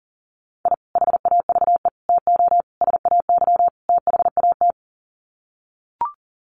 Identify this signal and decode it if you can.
Morse code, heard as a keyed tone, about 40 wpm, 700 Hz. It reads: I HAVE TO SAY THAT
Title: Sets of 5 Words Encore 40wpm